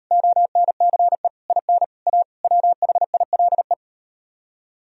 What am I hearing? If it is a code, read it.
Morse code, heard as a keyed tone, about 39 wpm, 700 Hz. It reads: ONCE IN A WHILE